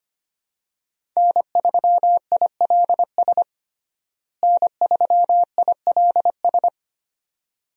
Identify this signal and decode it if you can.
Morse code, heard as a keyed tone, about 25 wpm, 700 Hz. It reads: N3ILS N3ILS
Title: USA Call Signs - 5 Characters - Repeated 2x - 25wpm